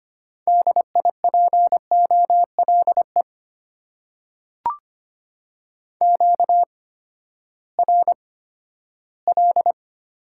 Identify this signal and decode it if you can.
Morse code, heard as a keyed tone, about 25 wpm, spaced as for 14 wpm, 700 Hz. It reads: DIPOLE QRL
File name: Top 90 QSO Elements - 8x Character Spacing - 25wpm